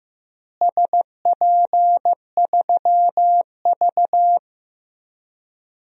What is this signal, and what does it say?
Morse code, heard as a keyed tone, about 15 wpm, 700 Hz. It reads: SP3V
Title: DXCC-100 Call Signs - 4 Characters - Repeated 3x - 15wpm